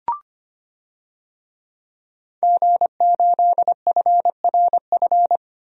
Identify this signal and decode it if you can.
Morse code, heard as a keyed tone, about 25 wpm, 700 Hz. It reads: G8FRF
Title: DXCC-100 Call Signs - 5 Characters - 25wpm